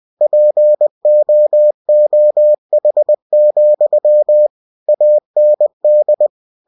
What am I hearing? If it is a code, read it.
Morse code, heard as a keyed tone, about 20 wpm, 600 Hz. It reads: POOH, AND
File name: cw-books / winnie-the-pooh / winnie-20wpm-page-0034